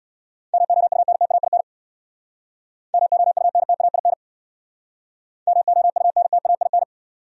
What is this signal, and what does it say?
Morse code, heard as a keyed tone, about 67 wpm, 700 Hz. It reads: MOUNTAIN MOUNTAIN MOUNTAIN